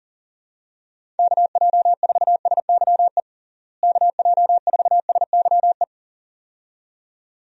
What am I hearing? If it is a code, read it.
Morse code, heard as a keyed tone, about 40 wpm, 700 Hz. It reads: KJ4SYE KJ4SYE
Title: USA Call Signs - 6 Characters - Repeated 2x - 40wpm